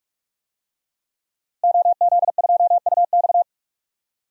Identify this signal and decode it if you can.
Morse code, heard as a keyed tone, about 45 wpm, 700 Hz. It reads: OZ2UX
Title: DXCC-100 Call Signs - 5 Characters - Repeated 3x - 45wpm